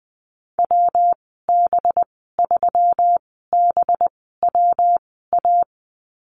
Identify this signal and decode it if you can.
Morse code, heard as a keyed tone, about 20 wpm, spaced as for 14 wpm, 700 Hz. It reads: WB3BWA
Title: 193 Lesson 31g - Letter v - Call Signs 20wpm